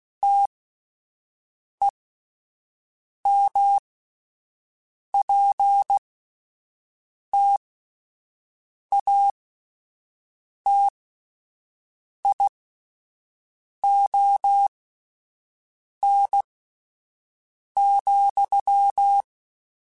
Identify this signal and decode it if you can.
Morse code, heard as a keyed tone, about 16 wpm, spaced as for 5 wpm, 800 Hz. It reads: TEMPTATION,
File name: Spaced / Chuckles Spaced-mp3s / Temptation sp @10